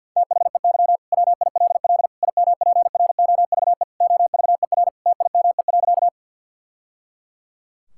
Morse code, heard as a keyed tone, about 50 wpm, 700 Hz. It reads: THEY WILL IMPROVE OVER TIME.
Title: Sentences from Top 1000 Words - Review Part 5 - 50wpm